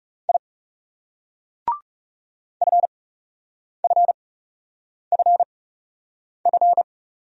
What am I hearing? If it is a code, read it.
Morse code, heard as a keyed tone, about 39 wpm, spaced as for 8 wpm, 700 Hz. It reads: I FFFF